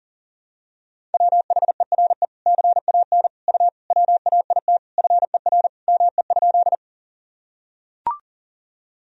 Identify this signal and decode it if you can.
Morse code, heard as a keyed tone, about 40 wpm, 700 Hz. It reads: WHERE CAN U WAIT FER ME?
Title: Sentences from Top 500 Words and Abbreviations 40wpm